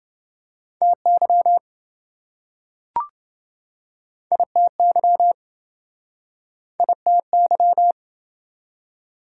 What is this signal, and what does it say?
Morse code, heard as a keyed tone, about 27 wpm, 700 Hz. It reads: TY ITY ITY